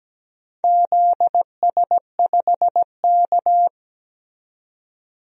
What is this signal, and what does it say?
Morse code, heard as a keyed tone, about 17 wpm, 700 Hz. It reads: ZS5K
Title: DXCC-100 Call Signs - 4 Characters - 17wpm